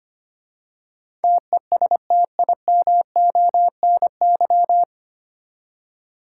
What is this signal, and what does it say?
Morse code, heard as a keyed tone, about 25 wpm, 700 Hz. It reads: TESTIMONY
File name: practice / 25wpm / 9 Letter Words - First 3 Letters Given - 25wpm